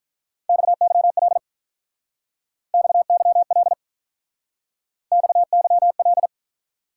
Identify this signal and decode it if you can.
Morse code, heard as a keyed tone, about 45 wpm, 700 Hz. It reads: XYL XYL XYL